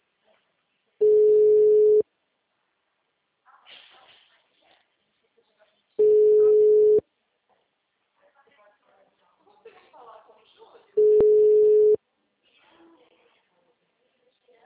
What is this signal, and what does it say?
Ringback tone (Europe, 425 Hz), with speech.
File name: out- -2012- - - .20296